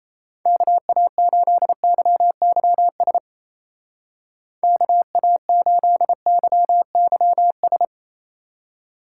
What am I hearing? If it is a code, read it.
Morse code, heard as a keyed tone, about 34 wpm, 700 Hz. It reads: KA8YYS KA8YYS
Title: Speed Racing - USA Call Signs 28wpm